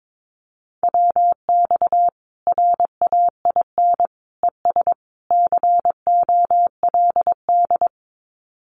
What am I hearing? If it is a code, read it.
Morse code, heard as a keyed tone, about 22 wpm, 700 Hz. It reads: WX RAIN ES COLD
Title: Common QSO Phrases 22wpm